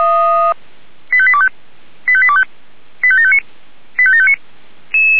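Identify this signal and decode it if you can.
Five-tone selective call (ZVEI), heard as quick runs of five tones, tones of about 70 ms. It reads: 85825 85825 85759 85759
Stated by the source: ZVEI five-tone selective call 85725 received via SDR.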